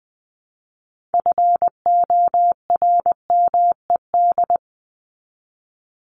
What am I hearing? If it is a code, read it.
Morse code, heard as a keyed tone, about 20 wpm, 700 Hz. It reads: FORMED
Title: ED Suffix - Top 100 Words with Base Words 20wpm